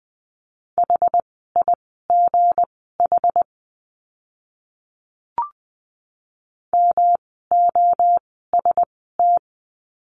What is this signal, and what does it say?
Morse code, heard as a keyed tone, about 20 wpm, spaced as for 15 wpm, 700 Hz. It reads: HIGH MOST
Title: 237 Lesson 38f - Letter q - Words 20wpm